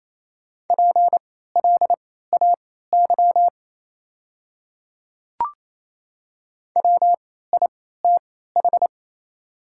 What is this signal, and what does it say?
Morse code, heard as a keyed tone, about 28 wpm, spaced as for 16 wpm, 700 Hz. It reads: PLAY WITH
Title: 3x Character Spacing - Top 200 Words - 28wpm